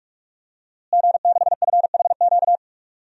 Morse code, heard as a keyed tone, about 41 wpm, 700 Hz. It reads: G6FHQ
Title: DXCC-100 Call Signs - 5 Characters - Repeated 3x - 45wpm